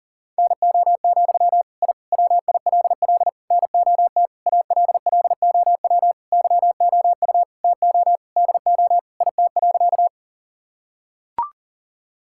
Morse code, heard as a keyed tone, about 40 wpm, 700 Hz. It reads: NO, I WILL NOT ALLOW YOU TO DO IT.
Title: Sentences from Top 1000 Words - Review Part 5 - 40wpm